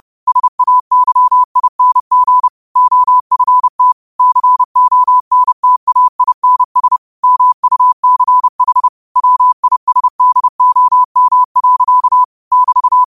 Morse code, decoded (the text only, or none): SAYING OFT CONTAINS MUCH WISDOM. <BT>